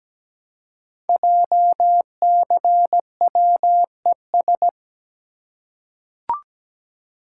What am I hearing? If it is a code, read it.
Morse code, heard as a keyed tone, about 17 wpm, 700 Hz. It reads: JCWES
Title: Code Groups - 5 Letters - 17wpm